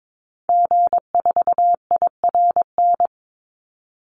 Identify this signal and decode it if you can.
Morse code, heard as a keyed tone, about 22 wpm, 700 Hz. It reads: G4IRN
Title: Call Signs Repeated 3x - 22wpm